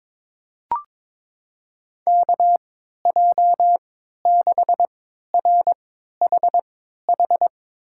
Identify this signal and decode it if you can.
Morse code, heard as a keyed tone, about 22 wpm, spaced as for 13 wpm, 700 Hz. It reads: KJ6RHH
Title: USA Call Signs - 6 Characters - 3x Character Spacing - 22wpm